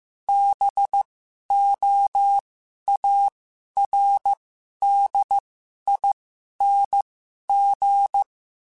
Morse code, heard as a keyed tone, about 15 wpm, spaced as for 11 wpm, 800 Hz. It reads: BOARDING
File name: Special INGs-mp3s / Special INGs-mp3s / Sngl Ing @10